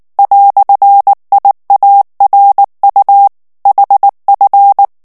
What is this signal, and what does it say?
Morse code, heard as a keyed tone, about 19 wpm, 800 Hz. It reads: "IARU HF
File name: morse mp3 / ALLEMAND texte clair 2 vitesse 18 mots minute